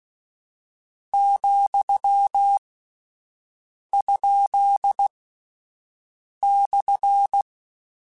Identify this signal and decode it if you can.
Morse code, heard as a keyed tone, about 16 wpm, 800 Hz. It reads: , ? /
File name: Misc / Misc-mp3s / Punct @10